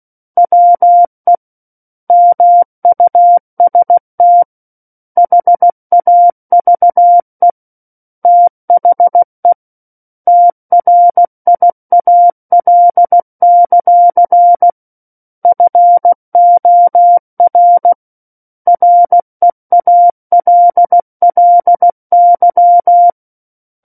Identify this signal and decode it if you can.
Morse code, heard as a keyed tone, about 16 wpm, 700 Hz. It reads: WE MUST HAVE THE TRIAL; FOR REALLY
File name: 03-Chapter-03 16 WPM